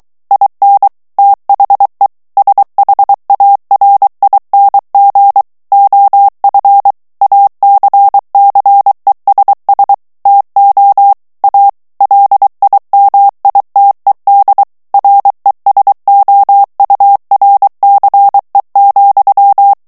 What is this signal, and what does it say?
Morse code, heard as a keyed tone, about 23 wpm, 800 Hz. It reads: IN THE SHARING OF ACCESS TO A LIMITED RESOURCE,
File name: ANGLAIS texte clair 3 vitesse 22 mots minute